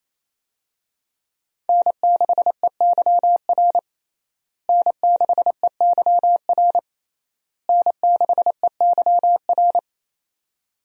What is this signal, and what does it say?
Morse code, heard as a keyed tone, about 27 wpm, 700 Hz. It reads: N6EYR N6EYR N6EYR
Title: USA Call Signs - 5 Characters - Repeated 3x - 28wpm